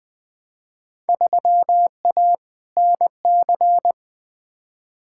Morse code, heard as a keyed tone, about 21 wpm, 700 Hz. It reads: 3A NC
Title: Field Day Exchanges - 20wpm